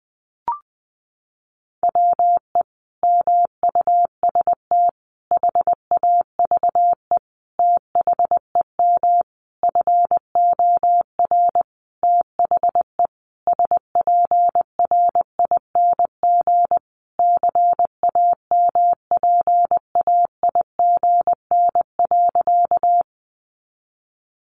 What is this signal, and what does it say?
Morse code, heard as a keyed tone, about 20 wpm, 700 Hz. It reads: WE MUST HAVE THEM FOR THE SPRING CAMPAIGN.
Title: Sentences from Top 1200 Words 20wpm